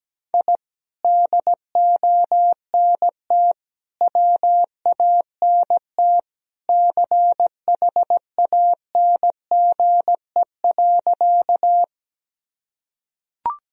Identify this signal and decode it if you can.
Morse code, heard as a keyed tone, about 17 wpm, 700 Hz. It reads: I DONT WANT CHANGE.
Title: Sentences from Top 1000 Words - Review Part 1 - 17wpm